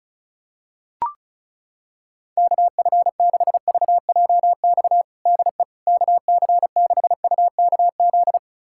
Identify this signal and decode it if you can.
Morse code, heard as a keyed tone, about 34 wpm, 700 Hz. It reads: KF6VJX DE KC6UKZ
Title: USA Callsign DE Callsign - 6 Characters - 35wpm